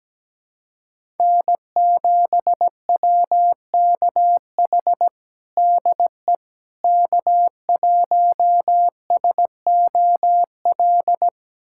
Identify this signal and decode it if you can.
Morse code, heard as a keyed tone, about 17 wpm, 700 Hz. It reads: N7WKH DE K1SOL
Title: USA Callsign DE Callsign - 5 Characters - 17wpm